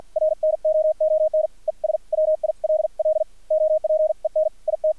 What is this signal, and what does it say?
Morse code, heard as a keyed tone, about 50 wpm, spaced as for 43 wpm, 620 Hz. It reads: RNOON ES WILL OPEN IT
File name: archived / CRSnet